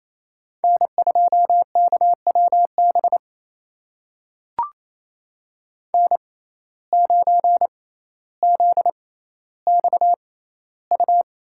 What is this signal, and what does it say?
Morse code, heard as a keyed tone, about 28 wpm, spaced as for 14 wpm, 700 Hz. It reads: N2KWB N9ZXU
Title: USA Call Signs - 5 Characters - 6x Character Spacing - 28wpm